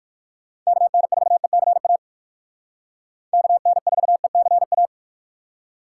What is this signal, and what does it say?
Morse code, heard as a keyed tone, about 54 wpm, 700 Hz. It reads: KN4ECA KN4ECA
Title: Speed Racing - USA 2x3 Call Signs 45wpm